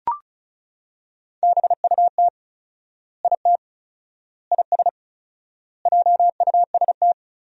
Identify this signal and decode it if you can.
Morse code, heard as a keyed tone, about 34 wpm, 700 Hz. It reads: BUT IT IS JUST
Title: 4x Word-Spacing - Sentences from Top 200 Words - 35wpm